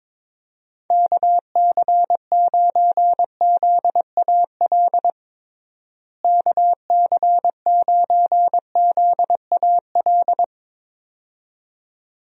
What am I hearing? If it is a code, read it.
Morse code, heard as a keyed tone, about 22 wpm, 700 Hz. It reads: KC9ZAL KC9ZAL
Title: USA Call Signs - 6 Characters - Repeated 2x - 22wpm